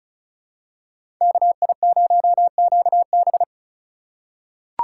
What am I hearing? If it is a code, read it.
Morse code, heard as a keyed tone, about 36 wpm, 700 Hz. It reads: KI0QB
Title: USA Call Signs - 5 Characters - 4x Character Spacing - 35wpm